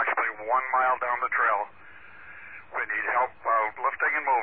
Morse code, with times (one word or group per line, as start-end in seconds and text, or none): none